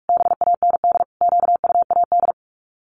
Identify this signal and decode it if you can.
Morse code, heard as a keyed tone, about 44 wpm, 700 Hz. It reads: BAND QUAD
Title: Common QSO Phrases 45wpm